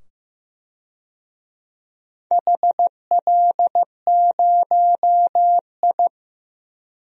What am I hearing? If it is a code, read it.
Morse code, heard as a keyed tone, about 15 wpm, 700 Hz. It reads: HL0I
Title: DXCC-100 Call Signs - 4 Characters - 15wpm